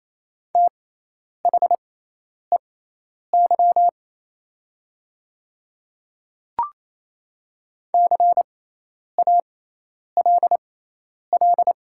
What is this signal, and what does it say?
Morse code, heard as a keyed tone, about 28 wpm, spaced as for 10 wpm, 700 Hz. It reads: THEY CALL
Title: 6x Character Spacing - Top 100 Words - 28wpm